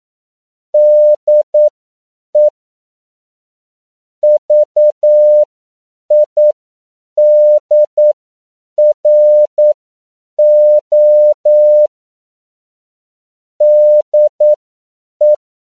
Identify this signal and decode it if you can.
Morse code, heard as a keyed tone, about 9 wpm, spaced as for 7 wpm, 600 Hz. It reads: DE VIDRO DE
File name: CW / 10PPM